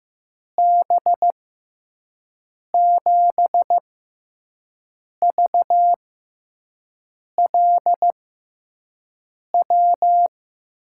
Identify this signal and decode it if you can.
Morse code, heard as a keyed tone, about 15 wpm, spaced as for 5 wpm, 700 Hz. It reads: B7VLW
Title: USA Call Signs - 6 Characters - 6x Character Spacing - 15wpm